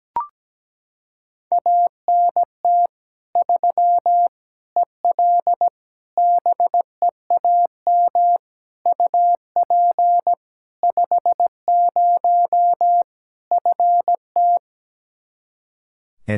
Morse code, heard as a keyed tone, about 17 wpm, 700 Hz. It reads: ANT 3 EL BEAM UP 50 FT
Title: Practice QSO - Exchange 2 - Info - 17wpm